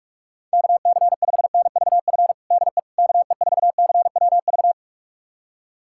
Morse code, heard as a keyed tone, about 42 wpm, 700 Hz. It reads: KC5NVF DE KE4CWV
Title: USA Callsign DE Callsign 45wpm